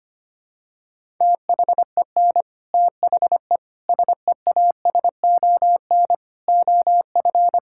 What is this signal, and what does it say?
Morse code, heard as a keyed tone, about 25 wpm, 700 Hz. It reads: THEN THE SEASON OF
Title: Sentences from Top 1600 Words 25wpm